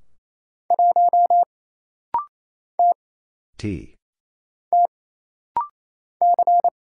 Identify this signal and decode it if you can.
Morse code, heard as a keyed tone, about 28 wpm, 700 Hz. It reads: 1 T T C
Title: Single Letter-Number Alt - Rapid-Fire 28wpm